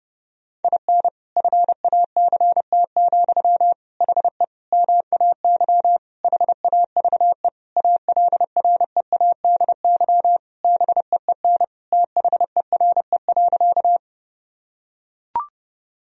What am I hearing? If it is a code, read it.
Morse code, heard as a keyed tone, about 29 wpm, 700 Hz. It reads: IN FACT, HE MAY HAVE ALREADY BEEN THERE.
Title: Sentences from Top 1000 Words - Review Part 3 - 30wpm